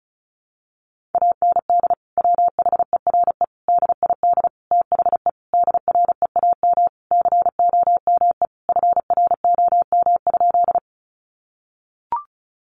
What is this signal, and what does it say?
Morse code, heard as a keyed tone, about 34 wpm, 700 Hz. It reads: AND WHERE DID THE DREAM COME FROM?
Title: Sentences from Top 1500 Words 35wpm